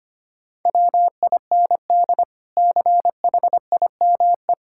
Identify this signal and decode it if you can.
Morse code, heard as a keyed tone, about 25 wpm, 700 Hz. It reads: WIND CHIME
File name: practice / 25wpm / Difficult Words with Hints - 25wpm